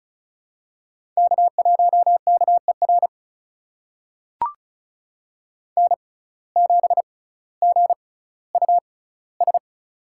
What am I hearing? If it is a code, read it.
Morse code, heard as a keyed tone, about 36 wpm, spaced as for 18 wpm, 700 Hz. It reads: K1KER N7GUS
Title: USA Call Signs - 5 Characters - 6x Character Spacing - 35wpm